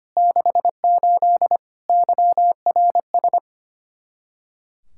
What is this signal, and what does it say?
Morse code, heard as a keyed tone, about 25 wpm, 700 Hz. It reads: 68 YRS
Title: Practice QSO - Exchange 3 - More Info - 25wpm